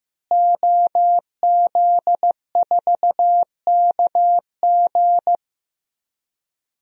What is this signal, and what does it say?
Morse code, heard as a keyed tone, about 15 wpm, 700 Hz. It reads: OZ4KG
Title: DXCC-100 Call Signs - 5 Characters - Repeated 3x - 15wpm